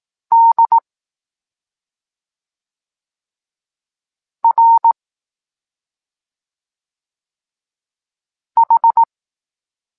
Morse code, heard as a keyed tone, about 18 wpm, 920 Hz. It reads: D R H
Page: Tekst 23.3 – Ciąg dalszy doskonalenia odbioru poznanych liter